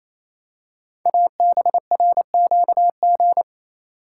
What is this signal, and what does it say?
Morse code, heard as a keyed tone, about 27 wpm, 700 Hz. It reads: ABRQG
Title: Code Groups - 5 Letters - 28wpm